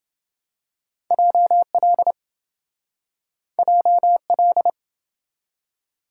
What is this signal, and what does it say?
Morse code, heard as a keyed tone, about 27 wpm, 700 Hz. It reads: JL JL